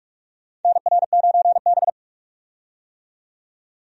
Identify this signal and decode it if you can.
Morse code, heard as a keyed tone, about 46 wpm, 700 Hz. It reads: NR9B
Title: USA Call Signs - 4 Characters - 45wpm